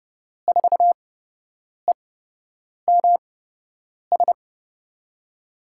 Morse code, heard as a keyed tone, about 30 wpm, spaced as for 8 wpm, 700 Hz. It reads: 4EMS
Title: USA Call Signs - 6 Characters - 8x Character Spacing - 30wpm